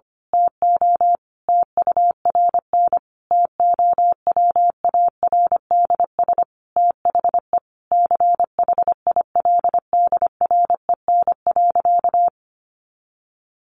Morse code, heard as a keyed tone, about 25 wpm, 700 Hz. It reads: TO TURN TOWARDS THE CHILDREN.
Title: Sentences from Top 1100 Words 25wpm